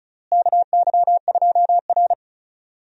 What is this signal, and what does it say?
Morse code, heard as a keyed tone, about 34 wpm, 700 Hz. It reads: KY2R
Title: USA Call Signs - 4 Characters - Repeated 3x - 35wpm